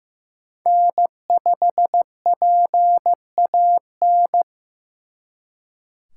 Morse code, heard as a keyed tone, about 18 wpm, 700 Hz. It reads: N5PAN
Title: Speed Racing - USA 1x3 Call Signs 15wpm